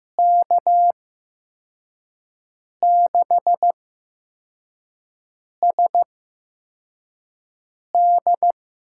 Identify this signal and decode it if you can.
Morse code, heard as a keyed tone, about 15 wpm, spaced as for 4 wpm, 700 Hz. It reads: K6SD
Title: USA Call Signs - 6 Characters - 8x Character Spacing - 15wpm